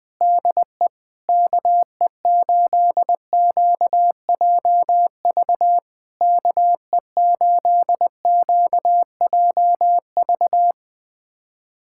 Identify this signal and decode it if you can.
Morse code, heard as a keyed tone, about 20 wpm, 700 Hz. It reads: DE KE8QJV KE8QJV
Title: POTA QSO Practice - 20wpm